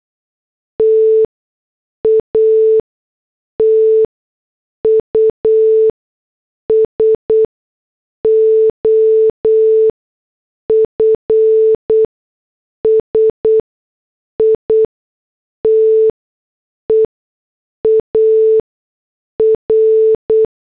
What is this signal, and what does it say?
Morse code, heard as a keyed tone, about 8 wpm, spaced as for 6 wpm, 440 Hz. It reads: TATUS OF SITE AR
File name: morse / headlines